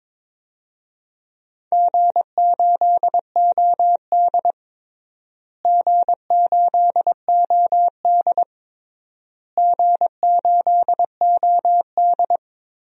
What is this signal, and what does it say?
Morse code, heard as a keyed tone, about 22 wpm, 700 Hz. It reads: G8OD G8OD G8OD
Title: DXCC-100 Call Signs - 4 Characters - Repeated 3x - 22wpm